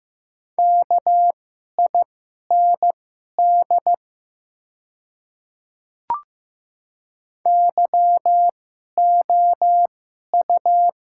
Morse code, heard as a keyed tone, about 15 wpm, spaced as for 11 wpm, 700 Hz. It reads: KIND YOU
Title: 2x Character Spacing - Top 500 Words - 15wpm